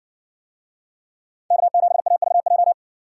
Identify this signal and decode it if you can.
Morse code, heard as a keyed tone, about 60 wpm, 700 Hz. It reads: K7AVJ